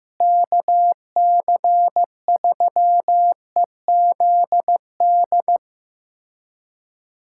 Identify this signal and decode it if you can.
Morse code, heard as a keyed tone, about 15 wpm, 700 Hz. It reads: KC3EZD
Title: USA Callsign DE Callsign 15wpm